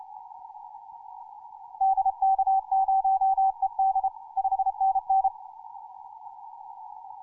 Morse code, heard as a keyed tone, about 29 wpm, 770 Hz. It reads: DK0ED 5NN
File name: simple-file-list / hk0na 10mcw